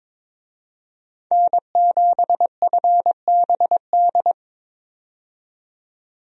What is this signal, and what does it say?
Morse code, heard as a keyed tone, about 22 wpm, 700 Hz. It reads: N7FBD
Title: USA Call Signs - 5 Characters - 22wpm